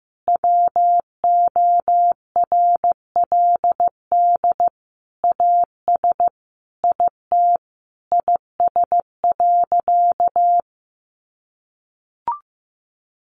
Morse code, heard as a keyed tone, about 15 wpm, 700 Hz. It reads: WORLD AS IT IS.
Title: Sentences from Top 1200 Words 15wpm